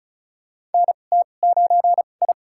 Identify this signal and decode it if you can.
Morse code, heard as a keyed tone, about 36 wpm, spaced as for 25 wpm, 700 Hz. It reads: NT9I
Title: USA Call Signs - 4 Characters - 2x Character Spacing - 35wpm